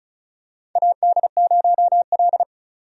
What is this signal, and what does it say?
Morse code, heard as a keyed tone, about 36 wpm, 700 Hz. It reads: AD0L
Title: USA Call Signs - 4 Characters - 35wpm